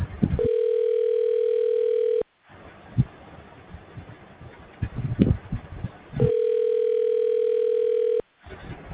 Ringback tone (North America), 440 + 480 Hz.